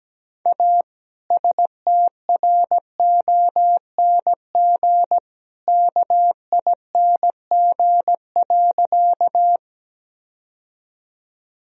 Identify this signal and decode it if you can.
Morse code, heard as a keyed tone, about 17 wpm, 700 Hz. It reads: A STRONG KING.
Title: Sentences from Top 1000 Words - Review Part 3 - 17wpm